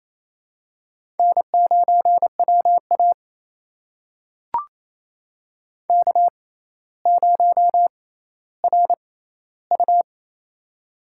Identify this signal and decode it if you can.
Morse code, heard as a keyed tone, about 28 wpm, spaced as for 14 wpm, 700 Hz. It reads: N9WA K0RU
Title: USA Call Signs - 4 Characters - 6x Character Spacing - 28wpm